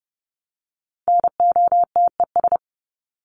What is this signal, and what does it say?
Morse code, heard as a keyed tone, about 30 wpm, 700 Hz. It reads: NOTES
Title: ES:S Suffix - Top 100 Words with Base Words 30wpm